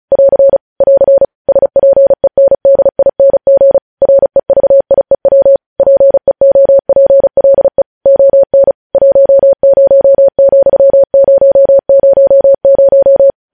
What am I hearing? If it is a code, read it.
Morse code, heard as a keyed tone, about 36 wpm, 550 Hz. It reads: <AR> <AR> SPENDING REVIEW PEOPLE ON 10,000
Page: news headlines in morse code at 35 wpm